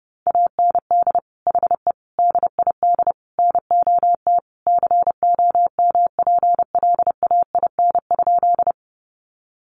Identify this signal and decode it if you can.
Morse code, heard as a keyed tone, about 28 wpm, 700 Hz. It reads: AND HE DID NOT COMPLAIN?
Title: Sentences from Top 1300 Words 30wpm